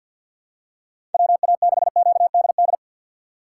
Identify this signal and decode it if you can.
Morse code, heard as a keyed tone, about 50 wpm, 700 Hz. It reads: WA6QDD
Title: USA Call Signs - 6 Characters - 6x Character Spacing - 50wpm